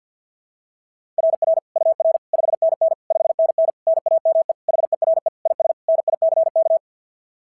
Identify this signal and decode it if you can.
Morse code, heard as a keyed tone, about 46 wpm, 660 Hz. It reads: RR UR 5NN 5NN NAME HERE IS NICK
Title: SOTA QSO Practice 50wpm